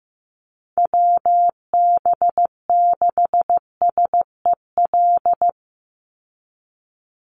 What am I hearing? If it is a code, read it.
Morse code, heard as a keyed tone, about 15 wpm, 700 Hz. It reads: WB6SEL
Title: Call Signs Repeated 3x - 15wpm